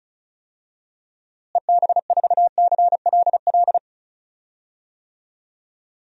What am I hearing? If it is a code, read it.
Morse code, heard as a keyed tone, about 35 wpm, 700 Hz. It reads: EB4CLL
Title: DXCC-100 Call Signs - 6 Characters - Repeated 3x - 35wpm